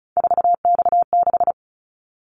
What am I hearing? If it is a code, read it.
Morse code, heard as a keyed tone, about 34 wpm, 700 Hz. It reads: 4X6
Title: Callsign Prefixes for Top 50 DXCC - 35wpm